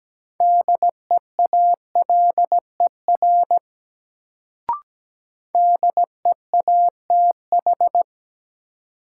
Morse code, heard as a keyed tone, about 17 wpm, 700 Hz. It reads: DEALER DEATH
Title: Sentences from Top 1100 Words - New Word Drill - 17wpm